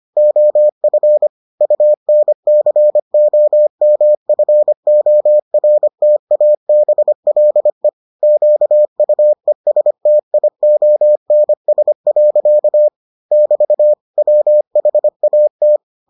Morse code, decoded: OF UNCOMFORTABLE QUESTIONS. <BT> WHAT

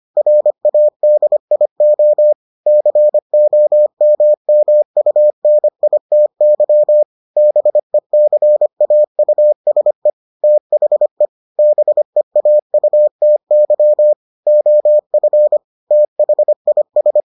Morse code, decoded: RADIO COMMUNITY BECAUSE THE BEAUTY OF THIS

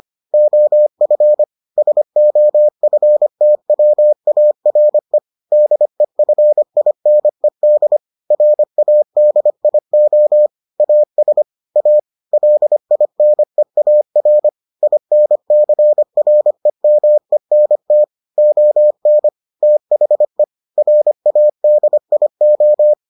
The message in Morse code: OF SOFTWARE DEFINED RADIO AS A LINEAR INCREMENT ON THE RADIO

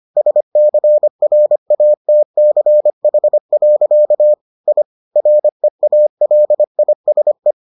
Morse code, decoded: SCRATCH. I REALISE

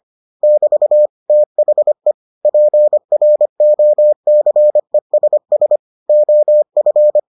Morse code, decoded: <BT> THE PROCESS OF

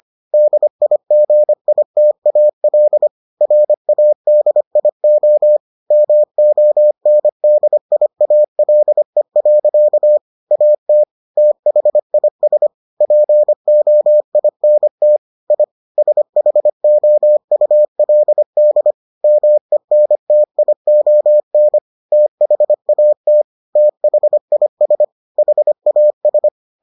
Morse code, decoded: DIGITAL RADIO MONDIALE. AT THIS POINT I SHOULD MENTION THAT THIS HAS

